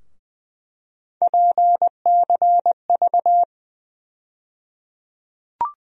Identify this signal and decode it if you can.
Morse code, heard as a keyed tone, about 20 wpm, 700 Hz. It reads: PCV
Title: USA Call Sign Postfixes - 3 Letters 20wpm